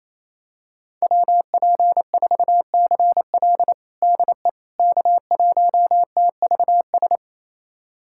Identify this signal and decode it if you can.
Morse code, heard as a keyed tone, about 27 wpm, 700 Hz. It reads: WP4CL DE K1TVS
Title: USA Callsign DE Callsign - 5 Characters - 28wpm